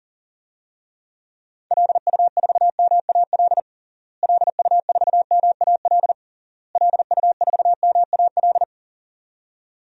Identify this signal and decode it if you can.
Morse code, heard as a keyed tone, about 40 wpm, 700 Hz. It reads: LU4MAL LU4MAL LU4MAL
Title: DXCC-100 Call Signs - 6 Characters - Repeated 3x - 40wpm